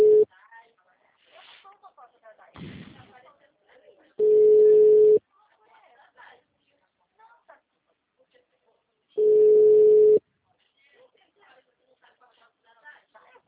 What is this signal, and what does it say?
Ringback tone (Europe, 425 Hz), with speech.
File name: out- -2008- - - .31194